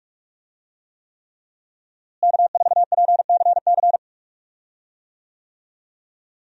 Morse code, heard as a keyed tone, about 47 wpm, 700 Hz. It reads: K4PCC
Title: USA Call Signs - 5 Characters - Repeated 3x - 45wpm